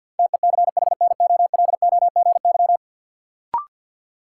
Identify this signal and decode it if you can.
Morse code, heard as a keyed tone, about 51 wpm, 700 Hz. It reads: TECHNOLOGY
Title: Sentences from Top 1000 Words - Review Part 5 - New Word Drill - 50wpm